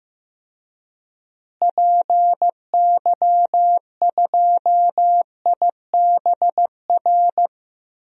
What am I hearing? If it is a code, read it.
Morse code, heard as a keyed tone, about 15 wpm, 700 Hz. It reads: PY2IBR
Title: DXCC-100 Call Signs - 6 Characters - Repeated 2x - 15wpm